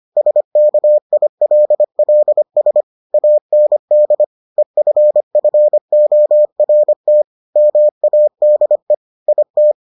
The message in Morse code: SKILLS AND EFFORT MADE IT

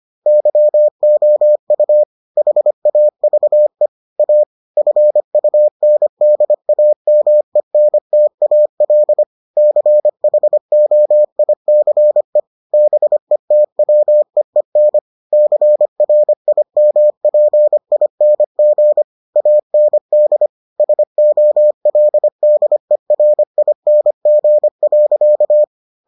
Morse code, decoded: YOU HAVE A FUNDAMENTAL CHOICE BETWEEN CRIMPING AND SOLDERING.

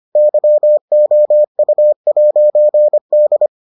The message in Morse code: YOU'D

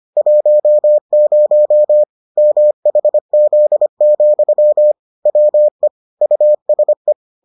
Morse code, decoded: 10 MHZ, WE USE

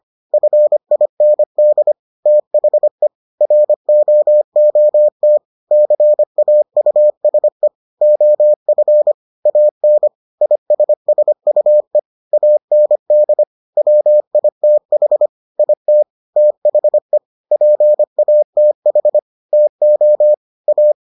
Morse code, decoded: FIND THE ROOT CAUSE OF AN ISSUE AND WITH IT THE PATH TO A

